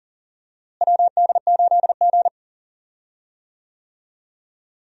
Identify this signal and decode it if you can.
Morse code, heard as a keyed tone, about 38 wpm, 700 Hz. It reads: WD8G
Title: USA Call Signs - 4 Characters - Repeated 2x - 40wpm